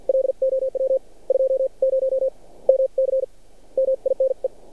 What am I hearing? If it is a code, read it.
Morse code, heard as a keyed tone, about 53 wpm, spaced as for 43 wpm, 530 Hz. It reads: LOW 20 MC MINE
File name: archived / CRSnet